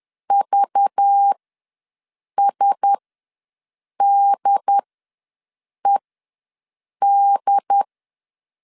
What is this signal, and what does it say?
Morse code, heard as a keyed tone, about 10 wpm, spaced as for 6 wpm, 800 Hz. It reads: VSDED
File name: LEZIONI / 5 LBVC / 5 LBVC ws 10wpm 3733